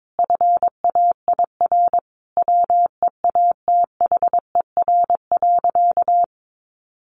Morse code, heard as a keyed tone, about 22 wpm, 700 Hz. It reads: FAIR WEATHER.
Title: Sentences from Top 1300 Words 22wpm